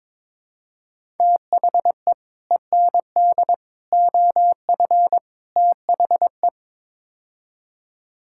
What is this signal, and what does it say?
Morse code, heard as a keyed tone, about 21 wpm, 700 Hz. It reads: THE END OF THE
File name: practice / 22wpm / Build A Phrase - 5-Word Phrases - 22wpm